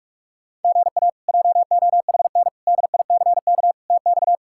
Morse code, heard as a keyed tone, about 45 wpm, 700 Hz. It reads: GA JOHN DICK TX
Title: K1USN SST QSO Practice - 45wpm